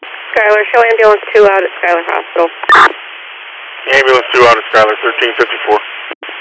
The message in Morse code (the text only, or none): none